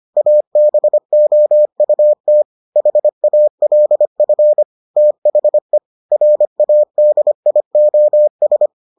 Morse code, heard as a keyed tone, about 25 wpm, 600 Hz. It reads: ABOUT HALF THE RADIOS